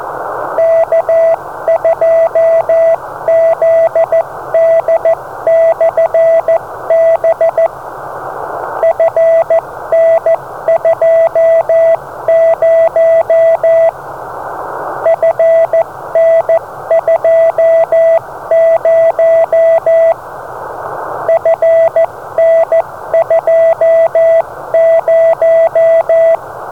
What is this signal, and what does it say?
Morse code, heard as a keyed tone, about 14 wpm, 660 Hz, with radio noise. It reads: K2ZD/B FN20 FN20 FN20
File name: beacons / K2ZD